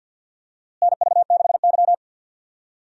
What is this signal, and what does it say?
Morse code, heard as a keyed tone, about 49 wpm, 700 Hz. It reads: NV6Y
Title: USA Call Signs - 4 Characters - Repeated 3x - 50wpm